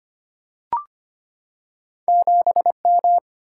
Morse code, heard as a keyed tone, about 25 wpm, 700 Hz. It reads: 7M
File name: Two Letter-Number Combinations - 25wpm